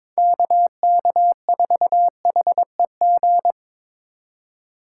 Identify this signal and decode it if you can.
Morse code, heard as a keyed tone, about 22 wpm, 700 Hz. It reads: KK4HEG
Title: K1USN SST QSO Practice - 22wpm